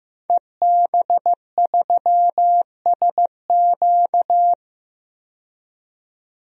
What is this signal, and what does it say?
Morse code, heard as a keyed tone, about 15 wpm, 700 Hz. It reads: EB3SQ
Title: DXCC-100 Call Signs - 5 Characters - 15wpm